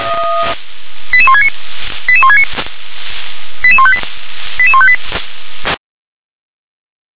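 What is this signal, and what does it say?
Five-tone selective call (ZVEI), heard as quick runs of five tones, tones of about 80 ms. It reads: 88169 88169 88158 88158
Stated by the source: ZVEI five-tone selective call received on SDR.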